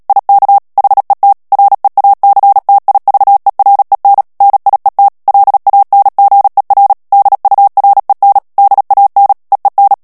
Morse code, heard as a keyed tone, about 37 wpm, 800 Hz. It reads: IK HET REACTIVEREN NIET LANGER DUREN DAN EEN
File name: morse mp3 / HOLLANDAIS texte clair 2 vitesse 35 mots minute